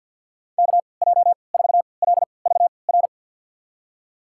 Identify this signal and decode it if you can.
Morse code, heard as a keyed tone, about 48 wpm, spaced as for 28 wpm, 690 Hz. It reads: KJ4LVR
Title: USA Call Signs - 6 Characters - 3x Character Spacing - 50wpm